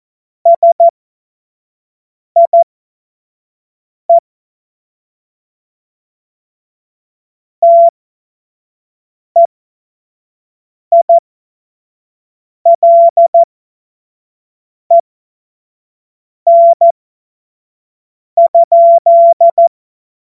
Morse code, heard as a keyed tone, about 14 wpm, spaced as for 5 wpm, 680 Hz. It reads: SIE TEILEN?
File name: Morsekurs / Witze / Witz 03 Juni 2025 25 F70